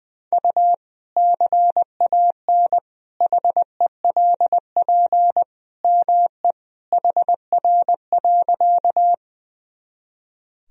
Morse code, heard as a keyed tone, about 20 wpm, 700 Hz. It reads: U CAN HELP ME HR.
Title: Sentences from Top 200 Words and Abbreviations 20wpm